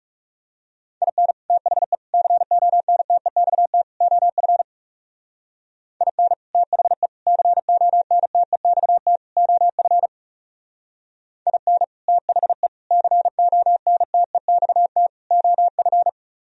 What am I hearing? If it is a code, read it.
Morse code, heard as a keyed tone, about 38 wpm, 700 Hz. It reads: IN THE CONTEXT OF IN THE CONTEXT OF IN THE CONTEXT OF